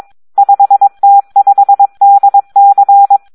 Morse code, heard as a keyed tone, about 22 wpm, 800 Hz. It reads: 5T5DC
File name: Audio practica CW / SD 1 22